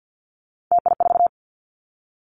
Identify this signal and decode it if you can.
Morse code, heard as a keyed tone, about 50 wpm, 700 Hz. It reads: TI4
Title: Callsign Prefixes for Top 100 DXCC - 50wpm